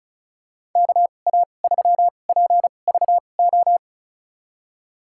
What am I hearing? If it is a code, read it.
Morse code, heard as a keyed tone, about 36 wpm, spaced as for 25 wpm, 690 Hz. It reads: KA3PVO
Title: USA Call Signs - 6 Characters - 2x Character Spacing - 35wpm